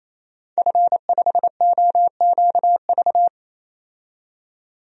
Morse code, heard as a keyed tone, about 28 wpm, 700 Hz. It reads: F5OQV
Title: DXCC-100 Call Signs - 5 Characters - 28wpm